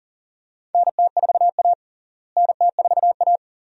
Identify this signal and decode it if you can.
Morse code, heard as a keyed tone, about 40 wpm, 700 Hz. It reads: NT4A NT4A
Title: USA Call Signs - 4 Characters - Repeated 2x - 40wpm